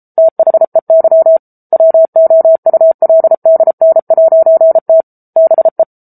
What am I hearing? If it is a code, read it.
Morse code, heard as a keyed tone, about 34 wpm, 650 Hz. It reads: THEY WOULDN'T BE
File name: Book Two / 07-Chapter 7 - 33 WPM